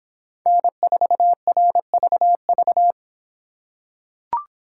Morse code, heard as a keyed tone, about 25 wpm, 700 Hz. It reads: N4RVV